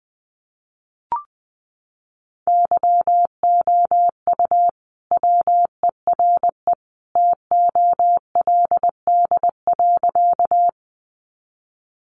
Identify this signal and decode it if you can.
Morse code, heard as a keyed tone, about 20 wpm, 700 Hz. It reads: YOU WERE TOLD.
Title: Sentences from Top 400 Words 20wpm